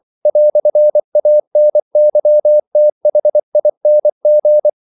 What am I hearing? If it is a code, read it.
Morse code, heard as a keyed tone, about 24 wpm, 600 Hz. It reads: "ANYTHING
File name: CW / Practice / EnglishQuotations0324